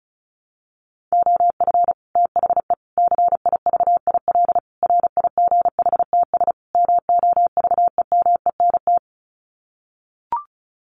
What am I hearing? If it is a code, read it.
Morse code, heard as a keyed tone, about 34 wpm, 700 Hz. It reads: OF THE CIVIL RIGHTS MOVEMENT
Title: Sets of 5 Words Encore2 35wpm